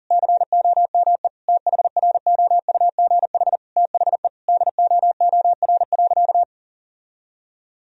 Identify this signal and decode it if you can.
Morse code, heard as a keyed tone, about 40 wpm, 700 Hz. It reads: COME THROUGH THE DOOR.
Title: Sentences from Top 1000 Words - Review Part 1 - 40wpm